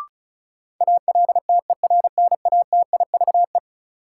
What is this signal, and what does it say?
Morse code, heard as a keyed tone, about 36 wpm, 700 Hz. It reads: ALTERNATIVE
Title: Sentences from Top 1000 Words - Review Part 4 - New Word Drill - 35wpm